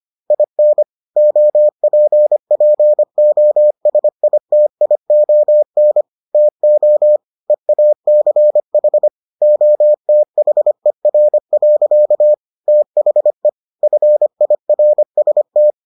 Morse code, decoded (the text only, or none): IN OPPOSITION TO EACH OTHER. THE FIRST